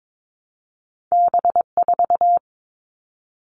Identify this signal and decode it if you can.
Morse code, heard as a keyed tone, about 22 wpm, 700 Hz. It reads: B4
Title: All Cut Numbers - 2-Digit 22wpm